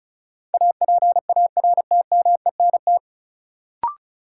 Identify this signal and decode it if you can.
Morse code, heard as a keyed tone, about 34 wpm, 700 Hz. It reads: APARTMENT
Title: Sentences from Top 900 Words - New Word Drill - 35wpm